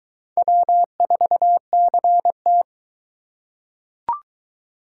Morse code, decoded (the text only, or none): W4CT